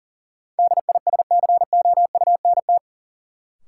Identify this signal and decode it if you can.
Morse code, heard as a keyed tone, about 40 wpm, 700 Hz. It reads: DISCOUNT
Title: Sentences from Top 1200 Words - New Word Drill Alt - 40wpm